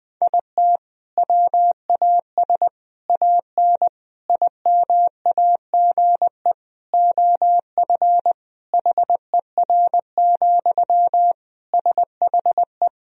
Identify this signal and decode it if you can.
Morse code, heard as a keyed tone, about 20 wpm, 700 Hz. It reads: IT WAS AN IMAGE OF HER, SHE
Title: Sentences from Top 1000 Words - Review Part 4 - 20wpm